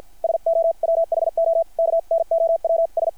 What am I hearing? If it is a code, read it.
Morse code, heard as a keyed tone, about 50 wpm, 660 Hz. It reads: SO WHO KNOWS
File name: archived / CRSnet